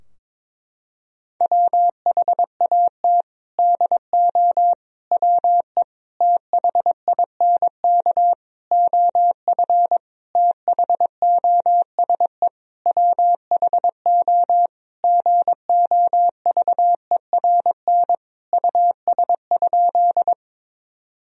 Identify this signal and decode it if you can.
Morse code, heard as a keyed tone, about 22 wpm, 700 Hz. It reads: WHAT DO WE THINK OF THOSE WHO GOVERN US?
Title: Sentences from Top 1000 Words - Review Part 2 - 22wpm